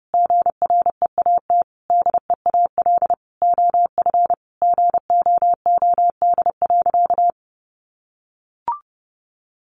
Morse code, heard as a keyed tone, about 30 wpm, 700 Hz. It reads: GREAT DEAL OF GOOD.
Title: Sentences from Top 900 Words 30wpm